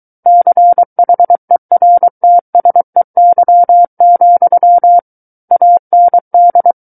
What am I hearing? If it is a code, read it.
Morse code, heard as a keyed tone, about 23 wpm, 700 Hz. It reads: CHERTSEY, AND
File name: Book One / 14-Chapter 14 - 23 WPM